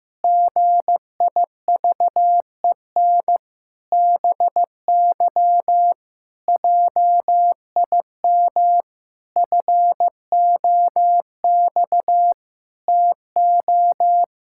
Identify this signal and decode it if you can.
Morse code, heard as a keyed tone, about 15 wpm, 700 Hz. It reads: GIVEN BY JIM FOX TO
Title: Pangram Challenge v2 - 15wpm